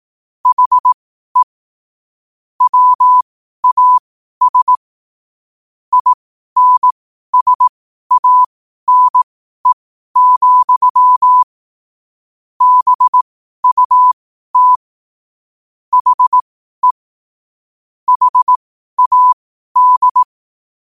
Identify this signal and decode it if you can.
Morse code, heard as a keyed tone, about 18 wpm, spaced as for 12 wpm, 1000 Hz. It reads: HE WAS INSANE, BUT HE HAD
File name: qotdfiles / 12wpm